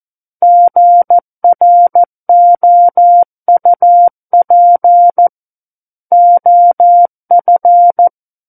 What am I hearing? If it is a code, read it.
Morse code, heard as a keyed tone, about 14 wpm, 700 Hz. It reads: GROUP OF
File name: Book One / 05-Chapter 5 - 14 WPM